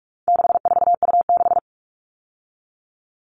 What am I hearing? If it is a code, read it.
Morse code, heard as a keyed tone, about 44 wpm, 700 Hz. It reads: 64U6
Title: All Cut Numbers - 4-Digit 45wpm